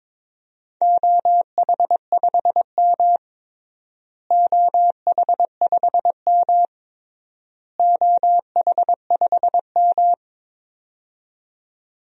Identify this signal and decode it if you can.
Morse code, heard as a keyed tone, about 22 wpm, 700 Hz. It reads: OH5M OH5M OH5M
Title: DXCC-100 Call Signs - 4 Characters - Repeated 3x - 22wpm